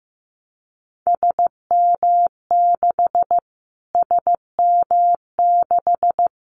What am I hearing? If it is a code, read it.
Morse code, heard as a keyed tone, about 15 wpm, 700 Hz. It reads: SM6 SM6
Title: Callsign Prefixes for Top 25 DXCC - Repeated 3x - 15wpm